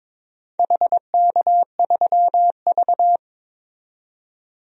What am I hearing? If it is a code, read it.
Morse code, heard as a keyed tone, about 22 wpm, 700 Hz. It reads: HK3V
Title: DXCC-100 Call Signs - 4 Characters - 22wpm